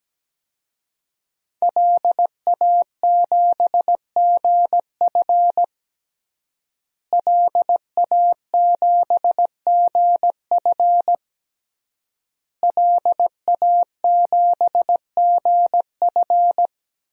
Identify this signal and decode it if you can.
Morse code, heard as a keyed tone, about 17 wpm, 700 Hz. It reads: LA7GF LA7GF LA7GF
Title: DXCC-100 Call Signs - 5 Characters - Repeated 3x - 17wpm